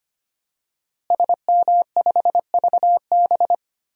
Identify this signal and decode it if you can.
Morse code, heard as a keyed tone, about 25 wpm, 700 Hz. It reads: SM5VB
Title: DXCC-100 Call Signs - 5 Characters - Repeated 3x - 25wpm